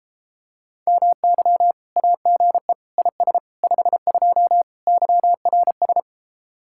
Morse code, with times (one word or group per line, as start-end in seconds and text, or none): 0.9-1.7 MY
2.0-2.7 AGE
3.0-3.4 IS
3.6-4.6 52
4.9-6.0 YRS